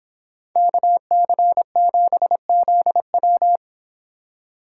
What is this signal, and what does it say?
Morse code, heard as a keyed tone, about 25 wpm, 700 Hz. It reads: KC7ZW